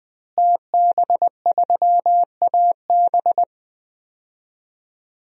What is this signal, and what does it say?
Morse code, heard as a keyed tone, about 20 wpm, 700 Hz. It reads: TB3AB
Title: DXCC-100 Call Signs - 5 Characters - Repeated 3x - 20wpm